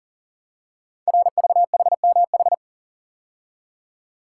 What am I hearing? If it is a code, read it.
Morse code, heard as a keyed tone, about 42 wpm, 690 Hz. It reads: RVHMH
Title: Code Groups - 5 Letters - 40wpm